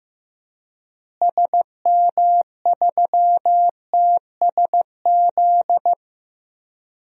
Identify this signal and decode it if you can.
Morse code, heard as a keyed tone, about 15 wpm, 700 Hz. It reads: SM3TSZ
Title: DXCC-100 Call Signs - 6 Characters - Repeated 2x - 15wpm